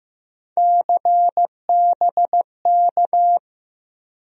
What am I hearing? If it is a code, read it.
Morse code, heard as a keyed tone, about 15 wpm, 700 Hz. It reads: CBK
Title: USA Call Sign Postfixes - 3 Letters 15wpm